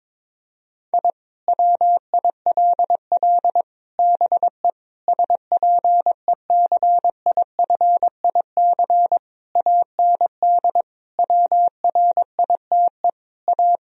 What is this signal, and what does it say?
Morse code, heard as a keyed tone, about 22 wpm, 700 Hz. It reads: I WILL BE SPECIFIC AND WRITE A
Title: Sentences from Top 1000 Words - Review Part 3 - 22wpm